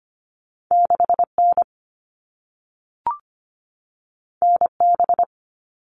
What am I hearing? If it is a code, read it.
Morse code, heard as a keyed tone, about 25 wpm, 700 Hz. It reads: 6N NB
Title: All Cut Numbers - 2-Digit 25wpm